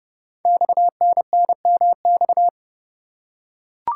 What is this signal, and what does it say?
Morse code, heard as a keyed tone, about 30 wpm, 700 Hz. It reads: XNNMX
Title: Code Groups - 5 Letters - 30wpm